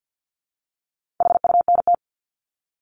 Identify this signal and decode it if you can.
Morse code, heard as a keyed tone, about 48 wpm, 700 Hz. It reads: HUNT
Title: Top 1501-1750 Words 50wpm